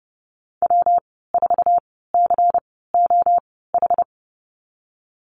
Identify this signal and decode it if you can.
Morse code, heard as a keyed tone, about 30 wpm, spaced as for 17 wpm, 700 Hz. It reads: W4COH
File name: course / 30:10wpm / 116 Lesson 19g - Letter w - Call Signs 30wpm